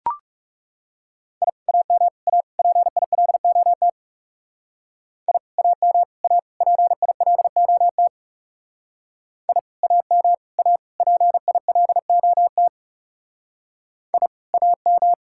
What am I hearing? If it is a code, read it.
Morse code, heard as a keyed tone, about 35 wpm, 700 Hz. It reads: I AM A PILOT I AM A PILOT I AM A PILOT I AM
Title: Speed-Racing Common QSO Phrases Encore - 30wpm